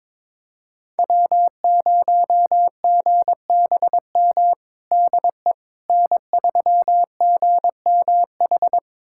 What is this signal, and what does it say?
Morse code, heard as a keyed tone, about 22 wpm, 700 Hz. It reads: W0GBM DE N3GMH
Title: USA Callsign DE Callsign - 5 Characters - 22wpm